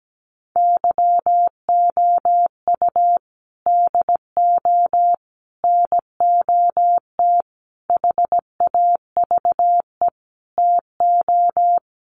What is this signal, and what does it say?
Morse code, heard as a keyed tone, about 17 wpm, 700 Hz. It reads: YOU DO NOT HAVE TO
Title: Sets of 5 Words 17wpm